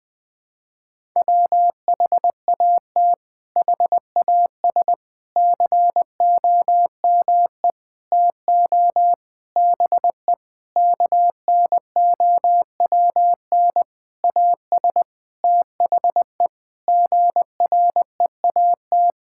Morse code, heard as a keyed tone, about 21 wpm, 700 Hz. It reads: WHAT HAS COME TO BE KNOWN AS THE GREAT
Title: Sentences with 8-Word Phrases - 20wpm